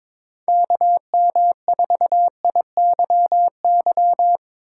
Morse code, heard as a keyed tone, about 25 wpm, 700 Hz. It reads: KM4IYY
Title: Speed Racing - USA Call Signs 22wpm